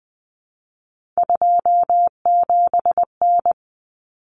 Morse code, heard as a keyed tone, about 20 wpm, 700 Hz. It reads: 27N
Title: Common Cut Numbers - 3-Digit 20wpm